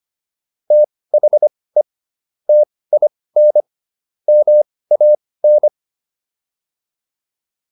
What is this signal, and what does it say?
Morse code, heard as a keyed tone, about 25 wpm, spaced as for 18 wpm, 600 Hz. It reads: THE TIN MAN